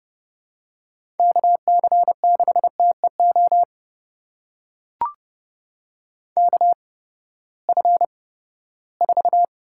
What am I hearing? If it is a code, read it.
Morse code, heard as a keyed tone, about 30 wpm, spaced as for 17 wpm, 700 Hz. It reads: KC6TEO KF4
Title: USA Call Signs - 6 Characters - 8x Character Spacing - 30wpm